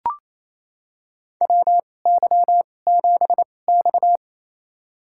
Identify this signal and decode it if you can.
Morse code, heard as a keyed tone, about 28 wpm, spaced as for 20 wpm, 700 Hz. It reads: WY7X
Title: USA Call Signs - 4 Characters - 2x Character Spacing - 28wpm